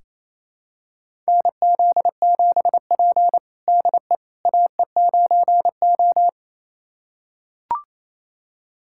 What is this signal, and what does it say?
Morse code, heard as a keyed tone, about 28 wpm, 700 Hz. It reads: NZ7P DE AE9O
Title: USA Callsign DE Callsign - 4 Characters - 28wpm